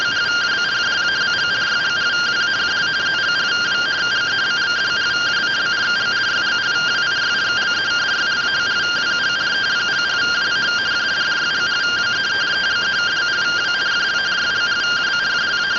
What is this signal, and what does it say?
Two-tone teleprinter signal, about 170 Hz shift.